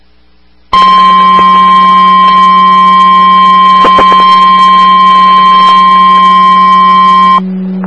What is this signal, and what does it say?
Emergency Alert System (SAME) tones (the 1050 Hz attention tone), with noise. It is heard from 0.7 to 7.4 s.
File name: RR Skywarn / W3MIE MHz Meadville ARESRACES SKYWARN Repeater